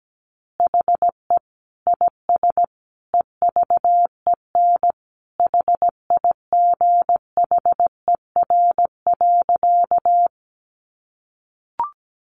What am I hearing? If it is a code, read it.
Morse code, heard as a keyed tone, about 17 wpm, 700 Hz. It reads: HE IS EVEN HIGHER.
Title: Additional Easy Phrases 17wpm